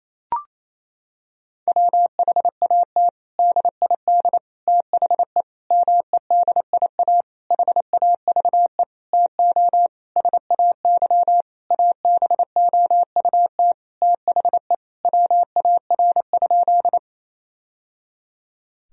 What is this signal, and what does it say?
Morse code, heard as a keyed tone, about 28 wpm, 700 Hz. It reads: WHAT DID THE MEDIA HAVE TO SAY ABOUT THE WAR?
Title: Sentences from Top 1000 Words - Review Part 4 - 28wpm